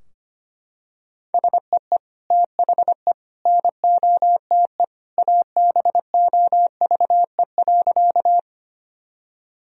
Morse code, heard as a keyed tone, about 25 wpm, 700 Hz. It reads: SEE THE NOTE ABOVE.
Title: Sentences from Top 1000 Words - Review Part 2 - 25wpm